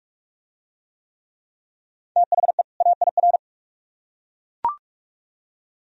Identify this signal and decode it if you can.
Morse code, heard as a keyed tone, about 40 wpm, 700 Hz. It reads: THE AIR